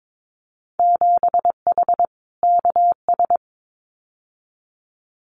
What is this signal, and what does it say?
Morse code, heard as a keyed tone, about 22 wpm, 700 Hz. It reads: 7H KS
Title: Winter Field Day Exchanges Encore 22wpm